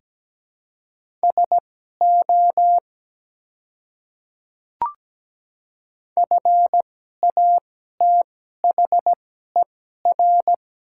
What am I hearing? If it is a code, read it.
Morse code, heard as a keyed tone, about 17 wpm, spaced as for 12 wpm, 700 Hz. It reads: SO FATHER
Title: 2x Character Spacing - Top 200 Words - 17wpm